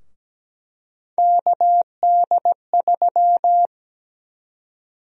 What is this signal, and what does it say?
Morse code, heard as a keyed tone, about 17 wpm, 700 Hz. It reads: KD3
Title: USA Call Sign Prefixes - 17wpm